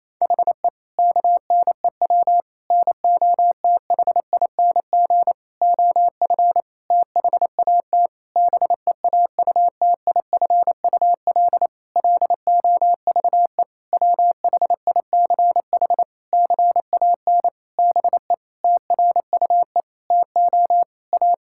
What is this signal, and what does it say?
Morse code, heard as a keyed tone, about 27 wpm, 700 Hz. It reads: HE KNEW NOTHING OF THAT BEAUTIFUL LOVE WHICH CAN BE TRUE TO A
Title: Sentences from Top 1600 Words 28wpm